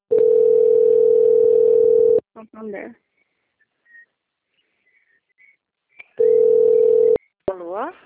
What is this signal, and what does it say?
Ringback tone (North America), 440 + 480 Hz, with speech.
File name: out- - - - - .46082